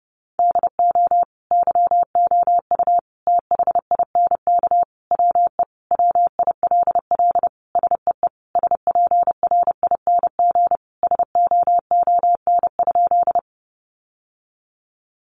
Morse code, heard as a keyed tone, about 30 wpm, 700 Hz. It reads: DO YOU THINK WE WILL SEE SPRING SOON?
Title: Sentences from Top 1200 Words 30wpm